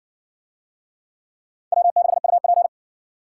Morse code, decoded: W6UP

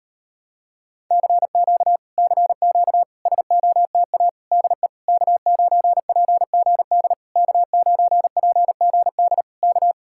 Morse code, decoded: CQ CQ SOTA DE K9PGD K9PGD K

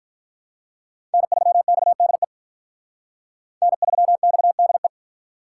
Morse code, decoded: N3XDE N3XDE

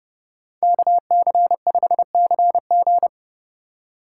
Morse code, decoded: KC5CG

